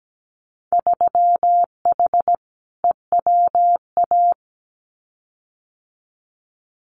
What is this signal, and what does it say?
Morse code, heard as a keyed tone, about 17 wpm, 700 Hz. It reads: 3H EWA
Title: Winter Field Day Exchanges Encore 17wpm